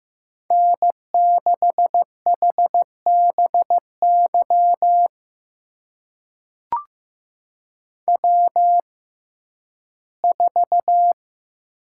Morse code, heard as a keyed tone, about 15 wpm, spaced as for 11 wpm, 700 Hz. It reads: N6HBY W4
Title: USA Call Signs - 5 Characters - 6x Character Spacing - 15wpm